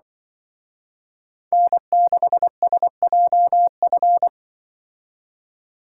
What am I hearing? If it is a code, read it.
Morse code, heard as a keyed tone, about 25 wpm, 700 Hz. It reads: N6SJF
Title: Speed Racing - USA 1x3 Call Signs 20wpm